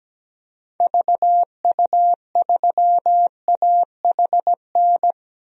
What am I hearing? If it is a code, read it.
Morse code, heard as a keyed tone, about 17 wpm, 700 Hz. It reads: VU3AHN
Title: DXCC-100 Call Signs - 6 Characters - 17wpm